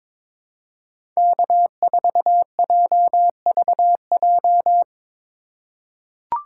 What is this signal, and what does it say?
Morse code, heard as a keyed tone, about 22 wpm, 700 Hz. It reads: K4JVJ
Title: USA Call Signs - 5 Characters - 2x Character Spacing - 22wpm